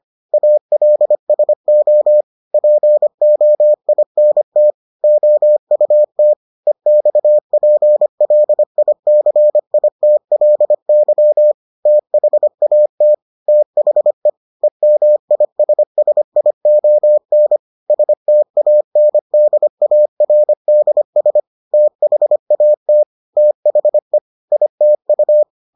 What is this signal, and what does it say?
Morse code, heard as a keyed tone, about 25 wpm, 600 Hz. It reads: ALSO POINT OUT EXPLICITLY THAT THE EMISSION STANDARDS THAT THE ITU